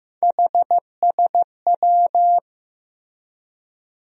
Morse code, heard as a keyed tone, about 15 wpm, 700 Hz. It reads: HSW
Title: USA Call Sign Postfixes - 3 Letters 15wpm